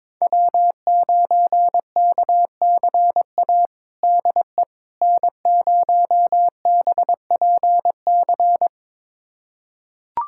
Morse code, heard as a keyed tone, about 22 wpm, 700 Hz. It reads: W9KCA DE N0BPC
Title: USA Callsign DE Callsign - 5 Characters - 22wpm